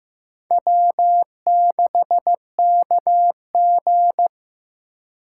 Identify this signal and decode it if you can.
Morse code, heard as a keyed tone, about 18 wpm, 700 Hz. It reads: W6KG
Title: Speed Racing - USA 1x2 Call Signs 15wpm